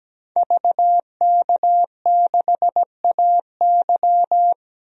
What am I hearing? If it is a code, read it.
Morse code, heard as a keyed tone, about 17 wpm, 700 Hz. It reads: VK6AY
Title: DXCC-100 Call Signs - 5 Characters - 17wpm